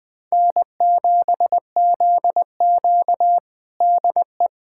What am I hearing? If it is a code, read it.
Morse code, heard as a keyed tone, about 20 wpm, 700 Hz. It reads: N7ZQ DE
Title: USA Callsign DE Callsign - 4 Characters - 20wpm